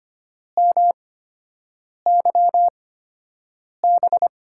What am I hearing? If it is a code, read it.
Morse code, heard as a keyed tone, about 25 wpm, spaced as for 7 wpm, 700 Hz. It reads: MYB
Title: USA Call Signs - 8x Character Spacing - 25wpm